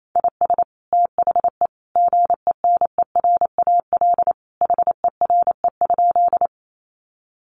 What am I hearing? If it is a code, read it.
Morse code, heard as a keyed tone, about 28 wpm, 700 Hz. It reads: IS THE GENERAL HERE?
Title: Sentences from Top 600 Words 28wpm